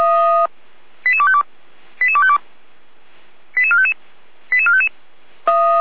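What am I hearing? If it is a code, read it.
Five-tone selective call (ZVEI), heard as quick runs of five tones, tones of about 70 ms. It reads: 88252 88252 88455 88455
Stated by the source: Five-tone ZVEI selective call received by SDR, from feuerwehr-orlamuende.org fire brigade.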